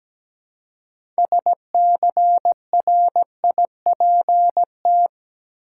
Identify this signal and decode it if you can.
Morse code, heard as a keyed tone, about 17 wpm, 700 Hz. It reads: SCRIPT
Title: Six Letter Words - 17wpm